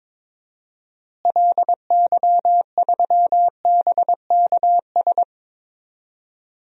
Morse code, heard as a keyed tone, about 21 wpm, 700 Hz. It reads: LY3BKS
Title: DXCC-100 Call Signs - 6 Characters - Repeated 3x - 22wpm